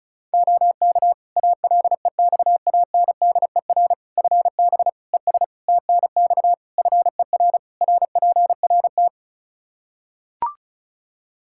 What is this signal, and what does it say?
Morse code, heard as a keyed tone, about 36 wpm, 700 Hz. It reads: OK ALEXANDER FB ES TNX FER RPRT
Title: K1USN SST QSO Practice 35wpm